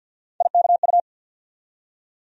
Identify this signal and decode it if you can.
Morse code, heard as a keyed tone, about 52 wpm, 690 Hz. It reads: IKU
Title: USA Call Sign Postfixes - 3 Letters 50wpm